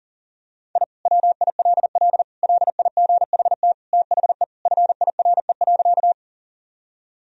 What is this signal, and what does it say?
Morse code, heard as a keyed tone, about 40 wpm, 700 Hz. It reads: I WILL LIGHT THE FIRE.
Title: Sentences from Top 1000 Words - Review Part 1 - 40wpm